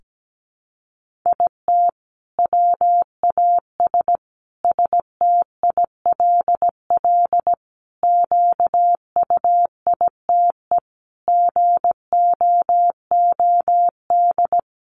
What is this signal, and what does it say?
Morse code, heard as a keyed tone, about 17 wpm, 700 Hz. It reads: IT WAS STILL QUITE GOOD
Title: Sentences from Top 1500 Words 17wpm